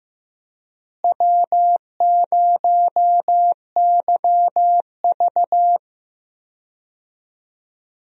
Morse code, heard as a keyed tone, about 15 wpm, 700 Hz. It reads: W0YV
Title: USA Call Signs - 4 Characters - Repeated 2x - 15wpm